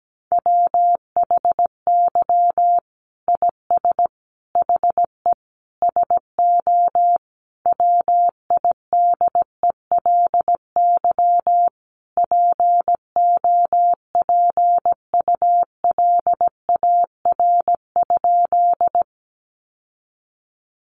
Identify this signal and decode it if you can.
Morse code, heard as a keyed tone, about 17 wpm, 700 Hz. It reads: WHY IS HE SO WIDELY POPULAR?
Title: Sentences from Top 1200 Words 17wpm